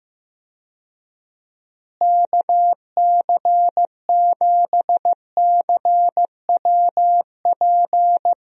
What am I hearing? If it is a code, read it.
Morse code, heard as a keyed tone, about 15 wpm, 700 Hz. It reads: KC7CWP
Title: USA Call Signs - 6 Characters - Repeated 3x - 15wpm